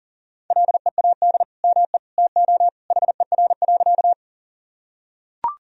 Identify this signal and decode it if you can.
Morse code, heard as a keyed tone, about 40 wpm, 700 Hz. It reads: LEAD ME TO HER.
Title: Sentences from Top 1000 Words - Review Part 2 - 40wpm